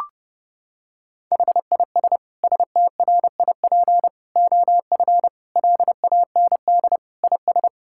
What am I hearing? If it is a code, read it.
Morse code, heard as a keyed tone, about 30 wpm, 700 Hz. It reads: HIS STRIP OF LAND IS
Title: Sentences from Top 1600 Words 30wpm